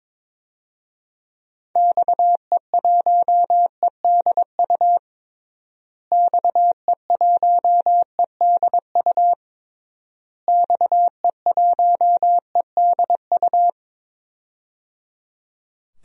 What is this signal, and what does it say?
Morse code, heard as a keyed tone, about 21 wpm, 700 Hz. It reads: XE1EDU XE1EDU XE1EDU
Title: DXCC-100 Call Signs - 6 Characters - Repeated 3x - 22wpm